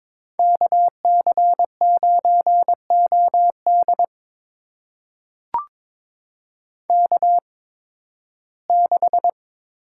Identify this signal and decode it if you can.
Morse code, heard as a keyed tone, about 23 wpm, spaced as for 15 wpm, 700 Hz. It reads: KC9OD K6
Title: USA Call Signs - 5 Characters - 8x Character Spacing - 22wpm